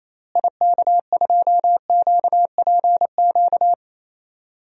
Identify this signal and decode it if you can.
Morse code, heard as a keyed tone, about 28 wpm, 700 Hz. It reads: IK2QPQ
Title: DXCC-100 Call Signs - 6 Characters - Repeated 2x - 28wpm